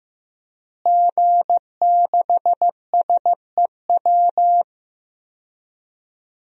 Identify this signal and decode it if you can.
Morse code, heard as a keyed tone, about 15 wpm, 700 Hz. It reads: G6SEW
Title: DXCC-100 Call Signs - 5 Characters - Repeated 2x - 15wpm